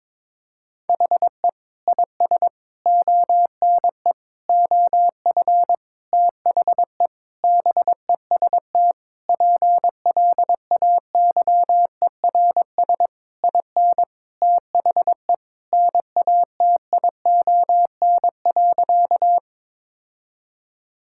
Morse code, heard as a keyed tone, about 22 wpm, 700 Hz. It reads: HE IS ONE OF THE BEST PLAYERS IN THE NATION.
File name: Sentences with 7-Word Phrases - 22wpm